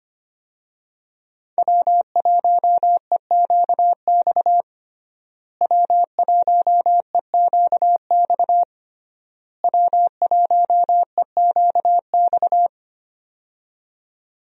Morse code, heard as a keyed tone, about 25 wpm, 700 Hz. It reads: W1EQX W1EQX W1EQX
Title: USA Call Signs - 5 Characters - Repeated 3x - 25wpm